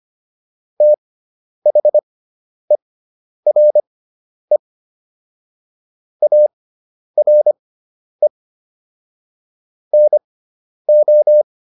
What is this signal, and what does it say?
Morse code, heard as a keyed tone, about 25 wpm, spaced as for 10 wpm, 600 Hz. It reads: THERE ARE NO